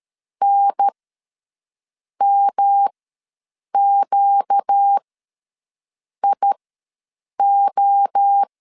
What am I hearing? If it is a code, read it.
Morse code, heard as a keyed tone, about 13 wpm, spaced as for 6 wpm, 800 Hz. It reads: NMQIO
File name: 5 LB ws 12wpm